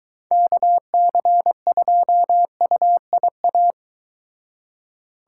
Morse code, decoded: KC2UIA

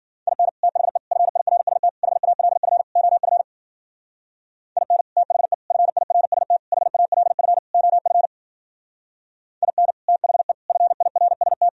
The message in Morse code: IN THE FIRST HALF OF IN THE FIRST HALF OF IN THE FIRST

